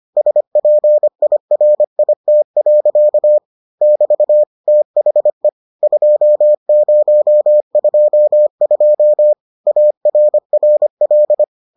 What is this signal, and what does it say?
Morse code, heard as a keyed tone, about 25 wpm, 600 Hz. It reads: SPIRIT. <BT> THE 2022 ARRL